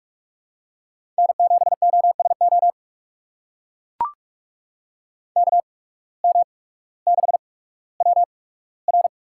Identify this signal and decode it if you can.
Morse code, heard as a keyed tone, about 44 wpm, spaced as for 19 wpm, 700 Hz. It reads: N7OSO KM6WR
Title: USA Call Signs - 5 Characters - 8x Character Spacing - 45wpm